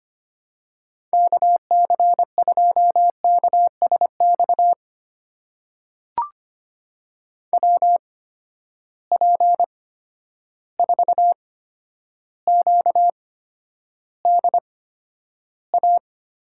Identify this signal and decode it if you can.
Morse code, heard as a keyed tone, about 25 wpm, spaced as for 11 wpm, 700 Hz. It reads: KC2KSX WP4QDA
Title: USA Call Signs - 8x Character Spacing - 25wpm